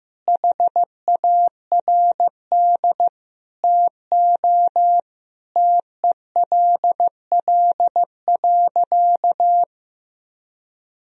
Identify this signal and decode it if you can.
Morse code, heard as a keyed tone, about 15 wpm, 700 Hz. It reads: HARD TO TELL.
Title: Sentences from Top 1000 Words - Review Part 1 - 15wpm